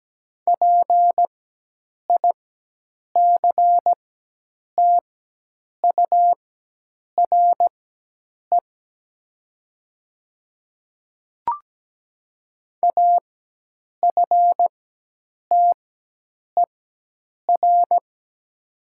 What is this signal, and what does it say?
Morse code, heard as a keyed tone, about 17 wpm, spaced as for 8 wpm, 700 Hz. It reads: PICTURE AFTER
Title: 4x Character Spacing - Top 200 Words - 17wpm